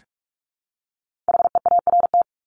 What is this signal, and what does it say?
Morse code, heard as a keyed tone, about 44 wpm, 700 Hz. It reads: HEART
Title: Top 1001-1250 Words 45wpm